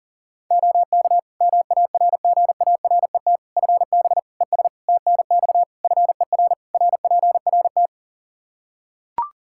Morse code, heard as a keyed tone, about 40 wpm, 700 Hz. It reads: OK MARGARET FB ES TNX FER RPRT
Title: Practice QSO - Exchange 2 - Info - 40wpm